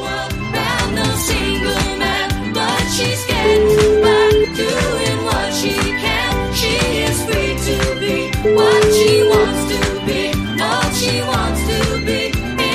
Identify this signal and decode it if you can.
Ringback tone (Europe, 425 Hz), with noise.